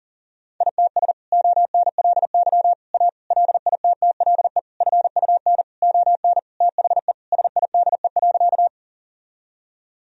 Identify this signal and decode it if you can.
Morse code, heard as a keyed tone, about 40 wpm, 700 Hz. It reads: ITS ONLY A LITTLE FUN ON THE SIDE.
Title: Sentences from Top 1000 Words - Review Part 4 - 40wpm